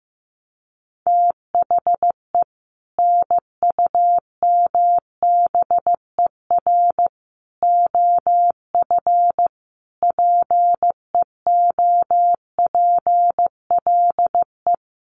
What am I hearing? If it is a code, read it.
Morse code, heard as a keyed tone, about 15 wpm, 700 Hz. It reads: THE NUMBER OF PEOPLE
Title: Sets of 5 Words Encore 15wpm